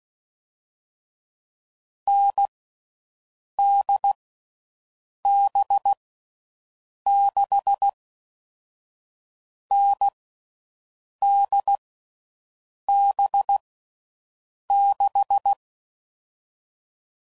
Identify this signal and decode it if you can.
Morse code, heard as a keyed tone, about 16 wpm, 800 Hz. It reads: N D B 6 N D B 6
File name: Combinations / Difficult Combos / NDB NDB6